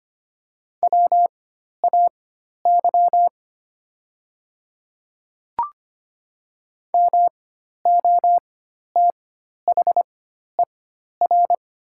Morse code, heard as a keyed tone, about 25 wpm, spaced as for 12 wpm, 700 Hz. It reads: WAY MOTHER
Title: 4x Character Spacing - Top 200 Words - 25wpm